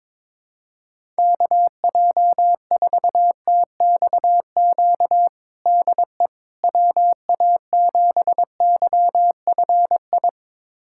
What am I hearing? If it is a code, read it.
Morse code, heard as a keyed tone, about 21 wpm, 700 Hz. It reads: KJ4TXQ DE WA7YFI
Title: USA Callsign DE Callsign - 6 Characters - 22wpm